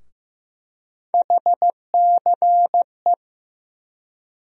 Morse code, heard as a keyed tone, about 15 wpm, 700 Hz. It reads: HCE
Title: USA Call Sign Postfixes - 3 Letters 15wpm